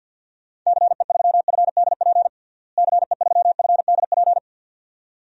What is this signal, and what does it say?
Morse code, heard as a keyed tone, about 50 wpm, 700 Hz. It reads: CE3FDP CE3FDP
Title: DXCC-100 Call Signs - 6 Characters - Repeated 2x - 50wpm